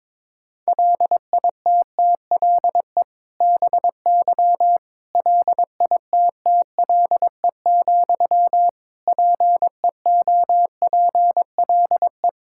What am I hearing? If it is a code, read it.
Morse code, heard as a keyed tone, about 22 wpm, 700 Hz. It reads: LITTLE BY LITTLE, PEOPLE
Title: Sentences from Top 1600 Words 22wpm